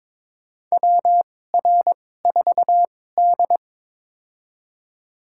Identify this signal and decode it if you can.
Morse code, heard as a keyed tone, about 22 wpm, spaced as for 16 wpm, 700 Hz. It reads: WR4D
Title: USA Call Signs - 4 Characters - 2x Character Spacing - 22wpm